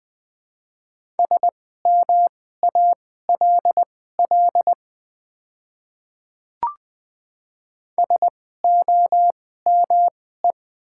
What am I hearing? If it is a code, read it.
Morse code, heard as a keyed tone, about 20 wpm, spaced as for 14 wpm, 700 Hz. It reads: SMALL SOME
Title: 2x Character Spacing - Top 300 Words - 20wpm